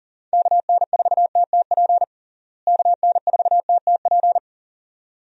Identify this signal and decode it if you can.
Morse code, heard as a keyed tone, about 40 wpm, 700 Hz. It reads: KN4TTP KN4TTP
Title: USA Call Signs - 6 Characters - Repeated 2x - 40wpm